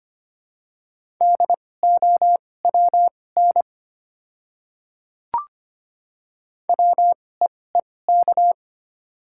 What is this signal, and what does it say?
Morse code, heard as a keyed tone, about 25 wpm, spaced as for 18 wpm, 700 Hz. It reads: DOWN WEEK
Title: 2x Character Spacing - Top 500 Words - 25wpm